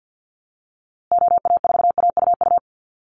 Morse code, heard as a keyed tone, about 50 wpm, 700 Hz. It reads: OA4AUU
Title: USA Call Signs Repeated 2x 50wpm